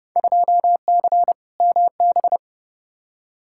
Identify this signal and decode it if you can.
Morse code, heard as a keyed tone, about 32 wpm, 700 Hz. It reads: 2C MB
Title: Field Day Exchanges Encore - 30wpm